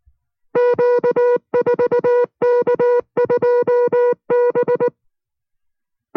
Morse code, heard as a keyed tone, about 19 wpm, 490 Hz. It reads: Q4K2B